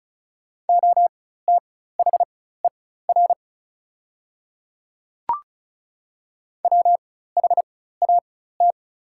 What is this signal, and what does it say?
Morse code, heard as a keyed tone, about 34 wpm, spaced as for 16 wpm, 690 Hz. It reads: OTHER WHAT
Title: 4x Character Spacing - Top 500 Words - 35wpm